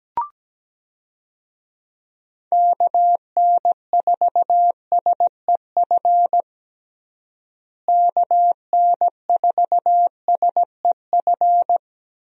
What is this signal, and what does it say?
Morse code, heard as a keyed tone, about 17 wpm, 700 Hz. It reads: KN4SEF KN4SEF
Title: USA Call Signs - 6 Characters - Repeated 2x - 17wpm